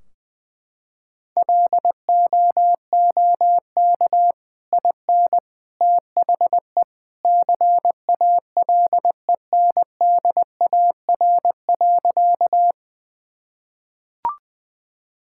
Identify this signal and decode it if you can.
Morse code, heard as a keyed tone, about 21 wpm, 700 Hz. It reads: LOOK IN THE CALENDAR.
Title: Sentences from Top 1600 Words 20wpm